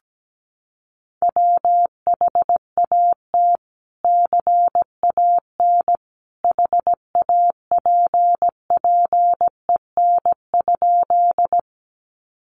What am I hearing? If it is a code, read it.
Morse code, heard as a keyed tone, about 17 wpm, 700 Hz. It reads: WHAT CAN HAPPEN?
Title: Sentences from Top 400 Words 17wpm